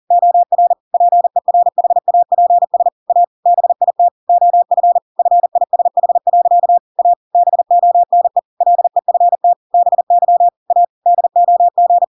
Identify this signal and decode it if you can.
Morse code, heard as a keyed tone, about 40 wpm, 700 Hz. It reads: OR PERHAPS A BIT OF FISH. A BONE LEFT BY A DOG
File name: Pinocchio / Pinocchio ch5 40 40 ik1hge